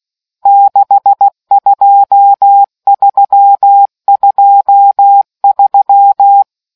Morse code, heard as a keyed tone, about 16 wpm, 800 Hz. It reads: 62323